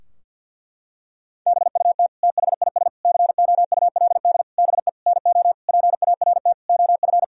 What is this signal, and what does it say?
Morse code, heard as a keyed tone, about 50 wpm, 700 Hz. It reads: BUT THIS COULD BE NO PART OF
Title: Sentences from Top 1000 Words - Review Part 2 - 50wpm